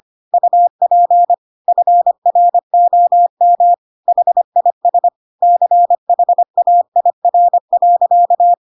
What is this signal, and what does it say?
Morse code, heard as a keyed tone, about 25 wpm, 700 Hz. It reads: UP FROM HIS CHAIR.